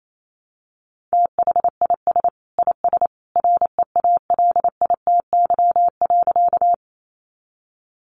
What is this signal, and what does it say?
Morse code, heard as a keyed tone, about 28 wpm, 700 Hz. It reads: THIS IS REALITY.
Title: Sentences from Top 1200 Words 28wpm